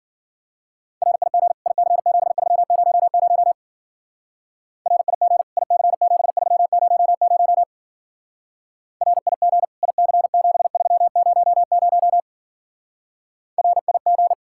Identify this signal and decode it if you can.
Morse code, heard as a keyed tone, about 47 wpm, 700 Hz. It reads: RIG IC7300 RIG IC7300 RIG IC7300 RIG
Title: Speed-Racing Common QSO Phrases - 40wpm